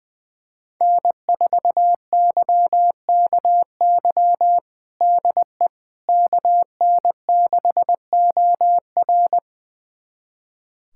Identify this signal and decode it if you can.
Morse code, heard as a keyed tone, about 20 wpm, 700 Hz. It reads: N4YKY DE KN6OR
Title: USA Callsign DE Callsign 20wpm